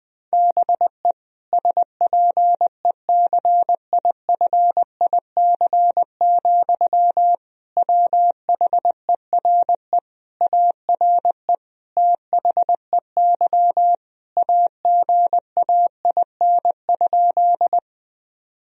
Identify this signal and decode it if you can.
Morse code, heard as a keyed tone, about 20 wpm, 700 Hz. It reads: BE SPECIFIC, WHERE ARE THEY AGAIN?
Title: Sentences from Top 1000 Words - Review Part 3 - 20wpm